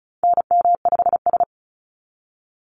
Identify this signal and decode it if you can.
Morse code, heard as a keyed tone, about 34 wpm, 700 Hz. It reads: NM5S
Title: NA SOTA Activator Callsigns 35wpm